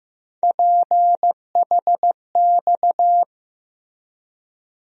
Morse code, heard as a keyed tone, about 15 wpm, 700 Hz. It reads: PHX
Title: USA Call Sign Postfixes - 3 Letters 15wpm